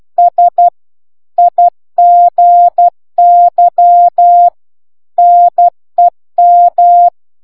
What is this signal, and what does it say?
Morse code, heard as a keyed tone, about 12 wpm, 700 Hz. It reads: S IGY NEM